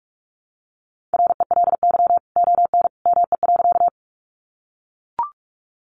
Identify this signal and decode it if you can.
Morse code, heard as a keyed tone, about 44 wpm, 700 Hz. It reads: RELY ON ME.
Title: Sentences from Top 1200 Words 45wpm